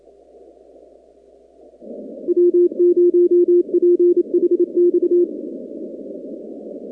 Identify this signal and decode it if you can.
Morse code, heard as a keyed tone, about 28 wpm, 350 Hz. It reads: W0PHX
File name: contests / 2019-cw-wpx-kt5j-k5pi- -2330 0-45